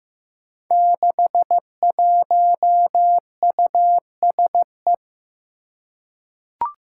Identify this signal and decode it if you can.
Morse code, heard as a keyed tone, about 15 wpm, 700 Hz. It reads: 61USE
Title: Five Challenging Letter-Number Combinations 15wpm